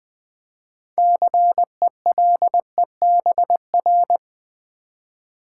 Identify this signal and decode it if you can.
Morse code, heard as a keyed tone, about 20 wpm, 700 Hz. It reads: CELEBR
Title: Build A Word - 3 Letters At A Time - 9-Letter Words - 20wpm